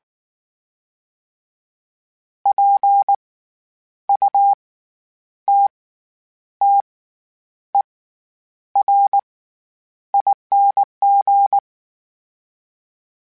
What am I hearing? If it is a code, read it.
Morse code, heard as a keyed tone, about 19 wpm, spaced as for 9 wpm, 800 Hz. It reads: PUTTERING
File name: Special INGs / Special INGs-mp3s / SlowToFaster ing @18